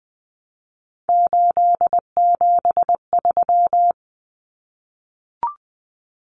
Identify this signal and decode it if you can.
Morse code, heard as a keyed tone, about 20 wpm, 700 Hz. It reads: 873
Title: Common Cut Numbers - 3-Digit 20wpm